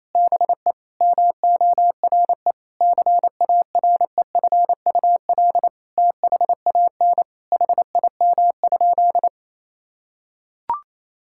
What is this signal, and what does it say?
Morse code, heard as a keyed tone, about 28 wpm, 700 Hz. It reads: BE MORE CAREFUL THAN HIM?
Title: Sentences from Top 1600 Words 28wpm